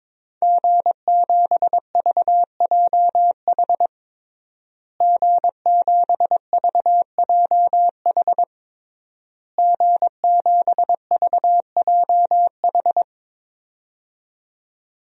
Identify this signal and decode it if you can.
Morse code, heard as a keyed tone, about 22 wpm, 700 Hz. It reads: G7VJH G7VJH G7VJH
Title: DXCC-100 Call Signs - 5 Characters - Repeated 3x - 22wpm